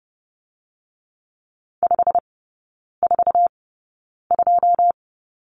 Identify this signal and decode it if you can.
Morse code, heard as a keyed tone, about 30 wpm, spaced as for 9 wpm, 700 Hz. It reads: 542
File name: course / 30:10wpm / 087 Lesson 15e - Number 5 - Three Characters 30wpm